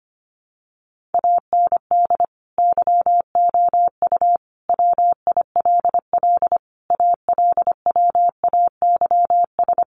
Morse code, heard as a keyed tone, about 25 wpm, 700 Hz. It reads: AND YOU WILL ALWAYS
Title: Sentences from Top 400 Words 25wpm